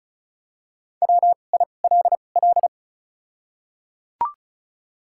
Morse code, heard as a keyed tone, about 36 wpm, spaced as for 25 wpm, 700 Hz. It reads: WILL
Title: 2x Character Spacing - Top 100 Words - 35wpm